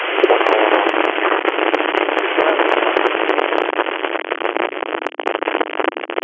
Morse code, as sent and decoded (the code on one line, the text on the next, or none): none